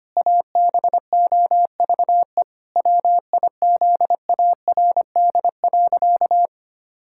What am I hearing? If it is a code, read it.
Morse code, heard as a keyed tone, about 25 wpm, 700 Hz. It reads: ABOVE WIZARD.
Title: Pangram Challenge v2 - 25wpm